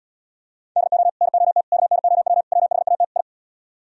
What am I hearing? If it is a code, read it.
Morse code, heard as a keyed tone, about 70 wpm, 700 Hz. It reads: DO NOT KNOW WHAT I